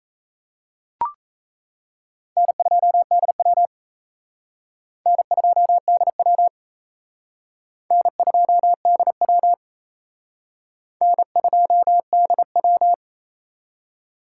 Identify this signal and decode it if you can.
Morse code, heard as a keyed tone, about 35 wpm, 700 Hz. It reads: N2DW N2DW N2DW N2DW